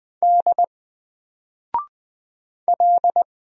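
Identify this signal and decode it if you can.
Morse code, heard as a keyed tone, about 20 wpm, 700 Hz. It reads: D L
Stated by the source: Single letter and number Morse drill at 20 wpm, from N4LCD.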